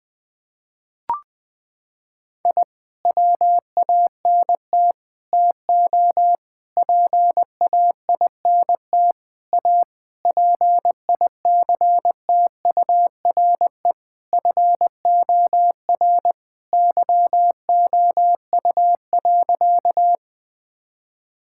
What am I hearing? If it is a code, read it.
Morse code, heard as a keyed tone, about 21 wpm, 700 Hz. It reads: I WANT TO PAINT A PICTURE FOR YOU.
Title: Sentences from Top 1000 Words - Review Part 6 - 20wpm